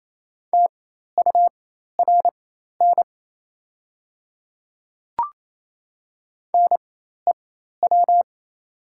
Morse code, heard as a keyed tone, about 28 wpm, spaced as for 13 wpm, 700 Hz. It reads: TURN NEW
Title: 4x Character Spacing - Top 200 Words - 28wpm